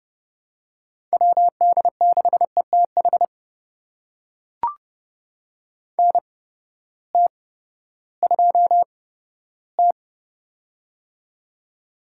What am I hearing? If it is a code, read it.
Morse code, heard as a keyed tone, about 30 wpm, spaced as for 15 wpm, 700 Hz. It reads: WD6ETH NT2T
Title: USA Call Signs - 8x Character Spacing - 30wpm